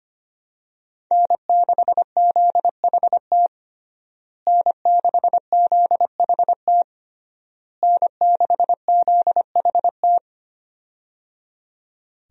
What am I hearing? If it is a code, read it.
Morse code, heard as a keyed tone, about 25 wpm, 700 Hz. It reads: N6ZHT N6ZHT N6ZHT
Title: USA Call Signs - 5 Characters - Repeated 3x - 25wpm